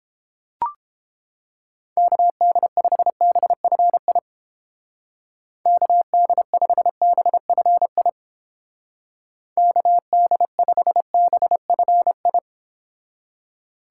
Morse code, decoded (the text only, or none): KD5BFI KD5BFI KD5BFI